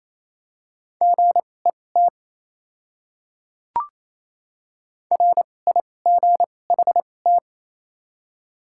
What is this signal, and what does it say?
Morse code, heard as a keyed tone, about 28 wpm, spaced as for 20 wpm, 700 Hz. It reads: GET RIGHT
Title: 2x Character Spacing - Top 300 Words - 28wpm